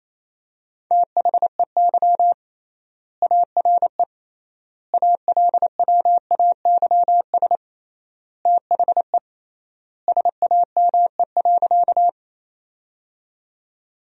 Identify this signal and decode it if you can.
Morse code, heard as a keyed tone, about 28 wpm, 700 Hz. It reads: THEY ARE ALWAYS THE SAME.
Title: 3x Word-Spacing - Sentences from Top 300 Words - 28wpm